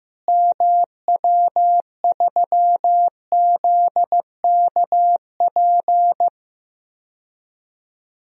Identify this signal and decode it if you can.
Morse code, heard as a keyed tone, about 16 wpm, 700 Hz. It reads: MW3ZKP
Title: DXCC-100 Call Signs - 6 Characters - Repeated 2x - 15wpm